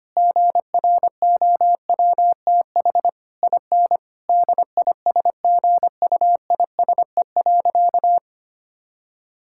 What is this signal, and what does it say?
Morse code, heard as a keyed tone, about 25 wpm, 700 Hz. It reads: GROWTH IN DISGUISE.
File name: practice / 25wpm / Difficult Sentences with Hints - 25wpm